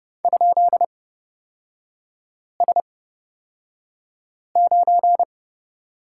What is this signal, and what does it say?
Morse code, heard as a keyed tone, about 30 wpm, spaced as for 5 wpm, 700 Hz. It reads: ?S9
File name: Lesson 38e - Letter q - Three Characters 30wpm